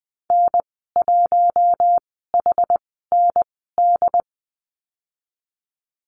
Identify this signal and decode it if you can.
Morse code, heard as a keyed tone, about 20 wpm, spaced as for 15 wpm, 700 Hz. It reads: N1HND
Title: 088 Lesson 15g - Number 5 - Call Signs 20wpm